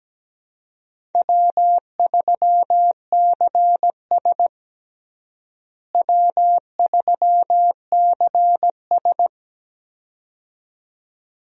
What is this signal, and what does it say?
Morse code, heard as a keyed tone, about 17 wpm, 700 Hz. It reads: W3CS W3CS
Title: USA Call Signs - 4 Characters - Repeated 2x - 17wpm